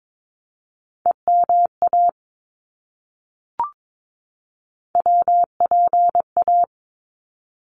Morse code, heard as a keyed tone, about 22 wpm, 700 Hz. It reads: EMA WPA
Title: ARRL Sections 22wpm